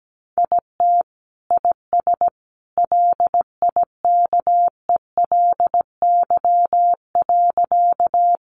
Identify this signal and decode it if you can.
Morse code, heard as a keyed tone, about 17 wpm, 700 Hz. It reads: IT IS LIKELY.
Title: Sentences from Top 600 Words 17wpm